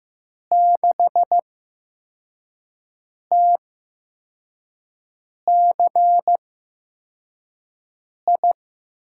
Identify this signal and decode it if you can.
Morse code, heard as a keyed tone, about 15 wpm, spaced as for 4 wpm, 700 Hz. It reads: 6TCI
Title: USA Call Signs - 6 Characters - 8x Character Spacing - 15wpm